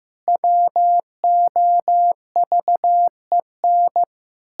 Morse code, heard as a keyed tone, about 15 wpm, 700 Hz. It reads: WOVEN
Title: Pangram Challenge v2 - 15wpm